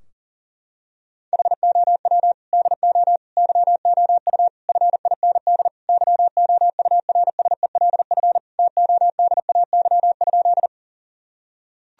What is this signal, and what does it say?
Morse code, heard as a keyed tone, about 40 wpm, 700 Hz. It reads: HOW DO YOU FIND YOURSELF TODAY?
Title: Sentences from Top 1000 Words - Review Part 4 - 40wpm